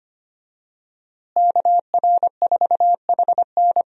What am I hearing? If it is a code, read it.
Morse code, heard as a keyed tone, about 25 wpm, 700 Hz. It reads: KR4HN
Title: USA Call Signs - 5 Characters - Repeated 3x - 25wpm